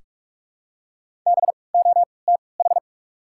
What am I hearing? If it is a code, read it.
Morse code, heard as a keyed tone, about 46 wpm, spaced as for 25 wpm, 700 Hz. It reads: BOTH
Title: 3x Character Spacing - Top 300 Words - 45wpm